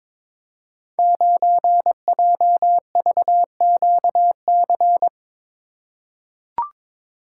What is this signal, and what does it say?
Morse code, heard as a keyed tone, about 21 wpm, 700 Hz. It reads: 9JVQC
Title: Code Groups - 5 Letter:Numbers - 22wpm